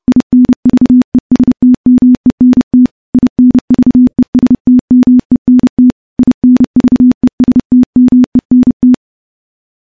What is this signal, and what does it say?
Morse code, heard as a keyed tone, about 30 wpm, 250 Hz. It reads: INVESTMENT INVESTMENT INVESTMENT
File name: morse / morse-training-30WPM